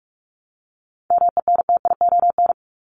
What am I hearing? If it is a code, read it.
Morse code, heard as a keyed tone, about 44 wpm, 700 Hz. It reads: MENTION
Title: Top 1501-1750 Words 45wpm